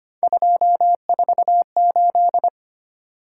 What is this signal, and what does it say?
Morse code, heard as a keyed tone, about 25 wpm, 700 Hz. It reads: 248
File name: Three Letter-Number Combinations - 25wpm